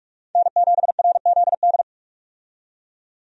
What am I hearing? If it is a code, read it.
Morse code, heard as a keyed tone, about 42 wpm, 700 Hz. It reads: N7RZD
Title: USA Call Signs - 5 Characters - 45wpm